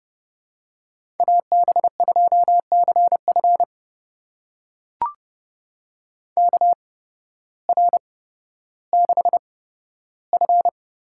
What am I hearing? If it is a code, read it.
Morse code, heard as a keyed tone, about 29 wpm, spaced as for 14 wpm, 700 Hz. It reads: AB2CF KR6F
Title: USA Call Signs - 5 Characters - 8x Character Spacing - 30wpm